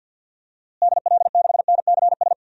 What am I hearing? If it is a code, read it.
Morse code, heard as a keyed tone, about 51 wpm, 700 Hz. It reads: DL6NCS
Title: DXCC-100 Call Signs - 6 Characters - Repeated 2x - 50wpm